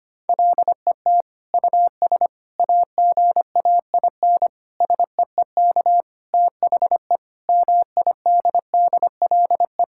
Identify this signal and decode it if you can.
Morse code, heard as a keyed tone, about 25 wpm, 700 Hz. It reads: LET US AGAIN SEEK THE MIDDLE
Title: Sentences from Top 1600 Words 25wpm